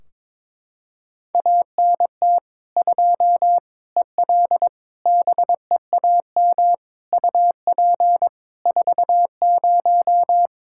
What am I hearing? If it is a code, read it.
Morse code, heard as a keyed tone, about 22 wpm, 700 Hz. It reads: ANT 2 EL BEAM UP 40
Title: Practice QSO - Exchange 2 - Info - 22wpm